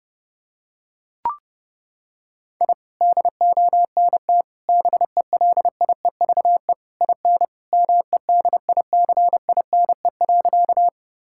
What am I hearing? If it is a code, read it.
Morse code, heard as a keyed tone, about 30 wpm, 700 Hz. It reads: I DONT BELIEVE IN MEDICINE.
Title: Sentences from Top 1000 Words - Review Part 6 - 30wpm